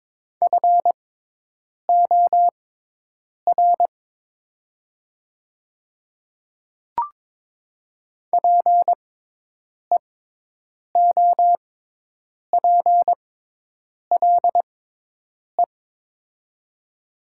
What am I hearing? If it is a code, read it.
Morse code, heard as a keyed tone, about 22 wpm, spaced as for 8 wpm, 700 Hz. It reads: FOR PEOPLE
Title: 6x Character Spacing - Top 200 Words - 22wpm